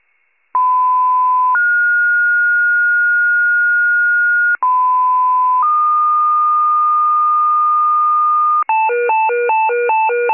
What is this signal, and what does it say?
Two-tone page, heard as a one-second tone, then a three-second tone, about 1010 then 1430 Hz, about 1010 then 1220 Hz, with noise. It is heard from 0.5 to 8.6 s.